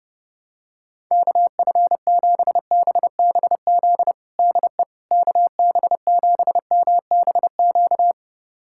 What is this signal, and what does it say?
Morse code, heard as a keyed tone, about 29 wpm, 700 Hz. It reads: KF7BBZ DE KB7MBQ
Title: USA Callsign DE Callsign - 6 Characters - 30wpm